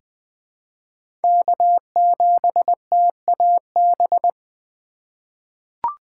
Morse code, heard as a keyed tone, about 20 wpm, 700 Hz. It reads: K7TAB
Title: 2023 SOTA Chaser Call Signs - 20wpm